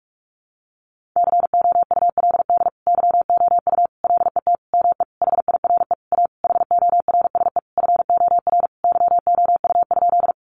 Morse code, heard as a keyed tone, about 44 wpm, 700 Hz. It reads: COULD YOU LET ME HIRE A HORSE FOR YOU?
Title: Sentences from Top 1400 Words 45wpm